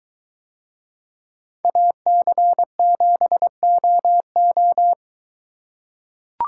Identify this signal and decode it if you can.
Morse code, heard as a keyed tone, about 22 wpm, 700 Hz. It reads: AC7OO